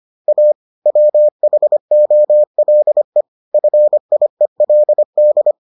Morse code, heard as a keyed tone, about 25 wpm, 600 Hz. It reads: A WHOLE FIELD